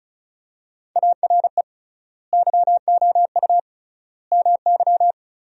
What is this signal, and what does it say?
Morse code, heard as a keyed tone, about 36 wpm, 700 Hz. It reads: ARE YOU MY
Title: 3x Word-Spacing - Sentences from Top 300 Words - 35wpm